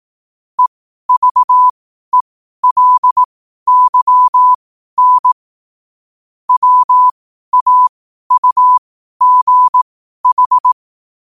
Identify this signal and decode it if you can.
Morse code, heard as a keyed tone, about 18 wpm, spaced as for 12 wpm, 1000 Hz. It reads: EVELYN WAUGH
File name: qotdfiles / 12wpm